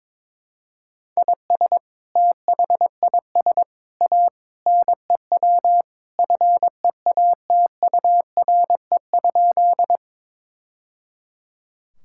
Morse code, heard as a keyed tone, about 22 wpm, 700 Hz. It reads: IS THIS A NEW FEATURE?
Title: Sentences from Top 1000 Words - Review Part 5 - 22wpm